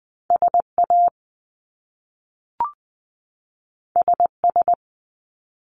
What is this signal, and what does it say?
Morse code, heard as a keyed tone, about 20 wpm, 700 Hz. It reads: SA SS
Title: Top Two Letter Combinations in Words 20wpm